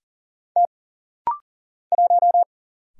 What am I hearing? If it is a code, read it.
Morse code, heard as a keyed tone, about 40 wpm, 700 Hz. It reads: T 1
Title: Single Letter-Number Alt - Rapid-Fire 40wpm